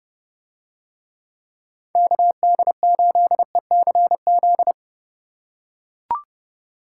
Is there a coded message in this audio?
Yes: Morse code from 2.0 to 4.7 s, heard as a keyed tone, about 28 wpm, 700 Hz. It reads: KD8ECZ